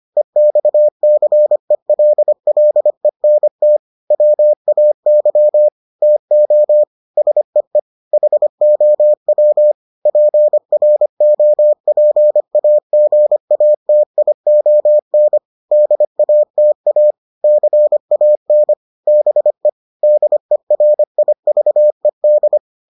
Morse code, heard as a keyed tone, about 25 wpm, 600 Hz. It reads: EXCELLENT WAY TO SEE HOW PROPAGATION DATA CAN BE DERIVED